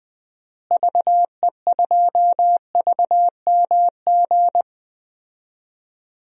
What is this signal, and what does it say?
Morse code, heard as a keyed tone, about 20 wpm, 700 Hz. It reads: VE2VMG
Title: DXCC-100 Call Signs - 6 Characters - Repeated 3x - 20wpm